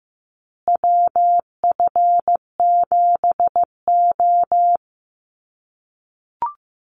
Morse code, heard as a keyed tone, about 15 wpm, 700 Hz. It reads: WF7O
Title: USA Call Signs Repeated 2x 15wpm